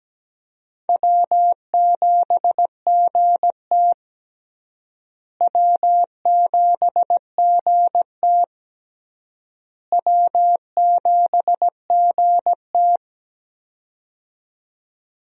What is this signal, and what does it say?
Morse code, heard as a keyed tone, about 17 wpm, 700 Hz. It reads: W7GT W7GT W7GT
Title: USA Call Signs - 4 Characters - Repeated 3x - 17wpm